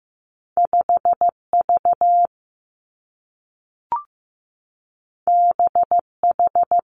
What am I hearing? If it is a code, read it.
Morse code, heard as a keyed tone, about 15 wpm, 700 Hz. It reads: 5 V B H
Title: Two Challenging Letter-Number Combinations 15wpm